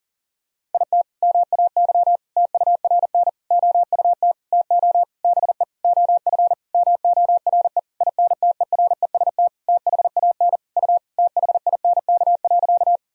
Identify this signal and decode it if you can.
Morse code, heard as a keyed tone, about 37 wpm, 700 Hz. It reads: IT MAY TURN OUT TO BE OF MORE INTEREST THAN U THINK.
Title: Sentences from Top 400 Words and Abbreviations 40wpm